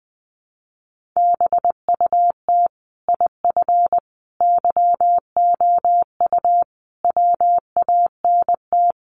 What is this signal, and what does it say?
Morse code, heard as a keyed tone, about 19 wpm, 700 Hz. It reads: BUT IF YOU WANT
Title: Sentences from Top 800 Words 20wpm